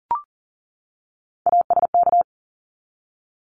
Morse code, heard as a keyed tone, about 40 wpm, 700 Hz. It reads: ASK
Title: ED Suffix - Top 100 Words with Base Words 40wpm